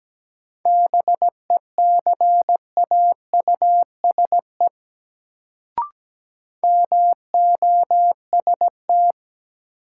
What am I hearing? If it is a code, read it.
Morse code, heard as a keyed tone, about 17 wpm, 700 Hz. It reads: BECAUSE MOST
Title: Sets of 2 Words - New Word Drill - 17wpm